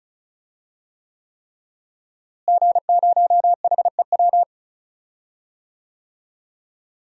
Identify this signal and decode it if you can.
Morse code, heard as a keyed tone, about 34 wpm, 700 Hz. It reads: G0HEW
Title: DXCC-100 Call Signs - 5 Characters - Repeated 3x - 35wpm